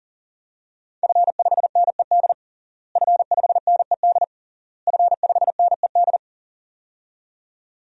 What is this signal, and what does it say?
Morse code, heard as a keyed tone, about 40 wpm, 700 Hz. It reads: F5NED F5NED F5NED
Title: DXCC-100 Call Signs - 5 Characters - Repeated 3x - 40wpm